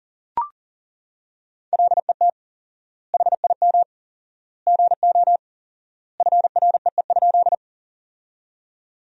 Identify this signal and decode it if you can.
Morse code, heard as a keyed tone, about 40 wpm, 700 Hz. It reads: LET HIM GO FREE?
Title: 4x Word-Spacing - Sentences from Top 500 Words - 40wpm